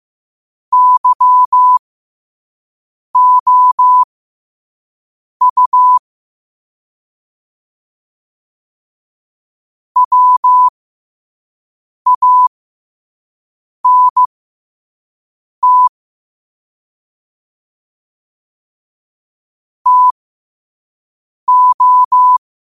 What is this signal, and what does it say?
Morse code, heard as a keyed tone, about 15 wpm, spaced as for 5 wpm, 1000 Hz. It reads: YOU WANT TO
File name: qotdfiles / 5wpm